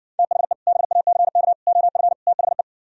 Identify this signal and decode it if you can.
Morse code, heard as a keyed tone, about 69 wpm, 700 Hz. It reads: THE BACK OF THE